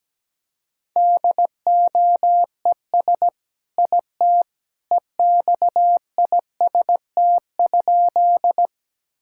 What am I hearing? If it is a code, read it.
Morse code, heard as a keyed tone, about 17 wpm, 700 Hz. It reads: DOES IT EXIST?
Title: Sentences from Top 1000 Words - Review Part 5 - 17wpm